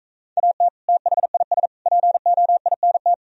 Morse code, heard as a keyed tone, about 43 wpm, 700 Hz. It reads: AT THIS POINT